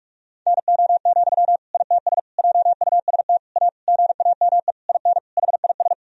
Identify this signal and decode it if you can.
Morse code, heard as a keyed tone, about 46 wpm, 700 Hz. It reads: NO, ITS JUST A GAME IN HIS
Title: Sentences from Top 1000 Words - Review Part 3 - 45wpm